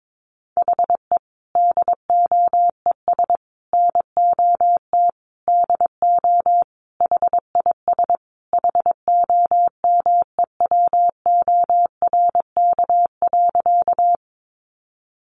Morse code, heard as a keyed tone, about 22 wpm, 700 Hz. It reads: HE DOES NOT DO HIS HOMEWORK.
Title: Additional Easy Phrases 22wpm